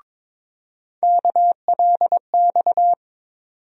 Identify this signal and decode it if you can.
Morse code, heard as a keyed tone, about 23 wpm, 700 Hz. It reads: KLX
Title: USA Call Sign Postfixes - 3 Letters 22wpm